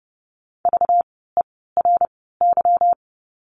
Morse code, heard as a keyed tone, about 30 wpm, spaced as for 17 wpm, 700 Hz. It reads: VERY
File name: course / 30:10wpm / 237 Lesson 38f - Letter q - Words 30wpm